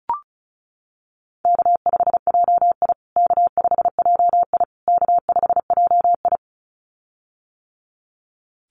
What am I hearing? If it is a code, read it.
Morse code, heard as a keyed tone, about 34 wpm, 700 Hz. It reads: K5JI K5JI K5JI
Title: USA Call Signs Repeated 3x 35wpm